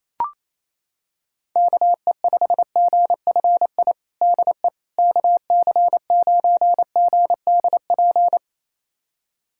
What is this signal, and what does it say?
Morse code, heard as a keyed tone, about 27 wpm, 700 Hz. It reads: KE5GFI DE KC9GDP
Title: USA Callsign DE Callsign 28wpm